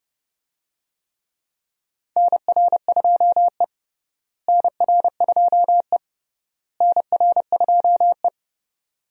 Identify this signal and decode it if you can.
Morse code, heard as a keyed tone, about 32 wpm, 700 Hz. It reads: NR2E NR2E NR2E
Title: USA Call Signs - 4 Characters - Repeated 3x - 30wpm